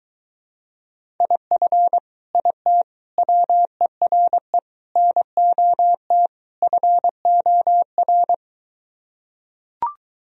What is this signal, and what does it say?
Morse code, heard as a keyed tone, about 23 wpm, 700 Hz. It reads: IF IT WERE NOT FOR